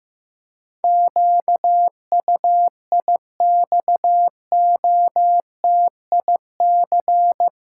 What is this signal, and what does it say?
Morse code, heard as a keyed tone, about 15 wpm, 700 Hz. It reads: QUIXOTIC
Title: Pangram Challenge v2 - 15wpm